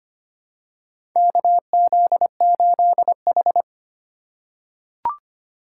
Morse code, heard as a keyed tone, about 25 wpm, 700 Hz. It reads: KZ8H
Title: USA Call Signs - 4 Characters - 4x Character Spacing - 25wpm